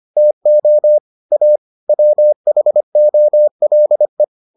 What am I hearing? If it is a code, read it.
Morse code, heard as a keyed tone, about 25 wpm, 600 Hz. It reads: TO A WHOLE